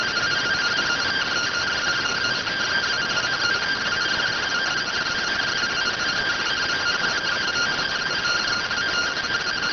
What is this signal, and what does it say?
Two-tone teleprinter signal, about 170 Hz shift.